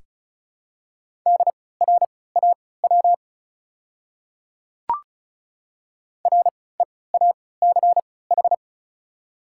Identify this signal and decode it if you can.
Morse code, heard as a keyed tone, about 34 wpm, spaced as for 20 wpm, 700 Hz. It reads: DRAW REACH
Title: 3x Character Spacing - Top 500 Words - 35wpm